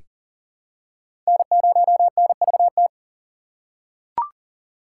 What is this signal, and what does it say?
Morse code, heard as a keyed tone, about 37 wpm, 700 Hz. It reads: N0NVT
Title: USA Call Signs - 5 Characters - 3x Character Spacing - 40wpm